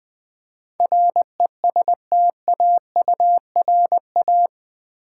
Morse code, heard as a keyed tone, about 20 wpm, 700 Hz. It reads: RESTAURA
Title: Build A Word - 4 Letters At A Time - 10-Letter Words - 20wpm